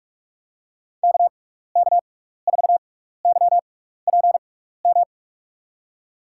Morse code, heard as a keyed tone, about 46 wpm, spaced as for 16 wpm, 700 Hz. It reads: KK4YPM
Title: USA Call Signs - 6 Characters - 6x Character Spacing - 45wpm